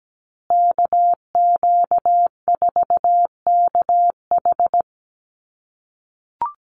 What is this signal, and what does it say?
Morse code, heard as a keyed tone, about 17 wpm, 700 Hz. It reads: KQ4KH
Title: USA Call Signs Repeated 2x 17wpm